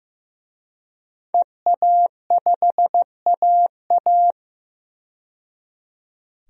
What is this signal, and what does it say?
Morse code, heard as a keyed tone, about 15 wpm, 700 Hz. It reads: EA5AA
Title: DXCC-100 Call Signs - 5 Characters - 15wpm